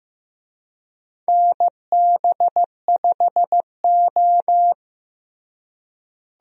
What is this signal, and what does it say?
Morse code, heard as a keyed tone, about 15 wpm, 700 Hz. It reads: NB5O
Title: USA Call Signs - 4 Characters - Repeated 2x - 15wpm